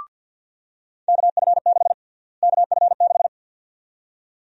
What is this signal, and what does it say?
Morse code, heard as a keyed tone, about 45 wpm, 700 Hz. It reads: KF6 KF6
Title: USA Call Sign Prefixes - Repeated 2x - 50wpm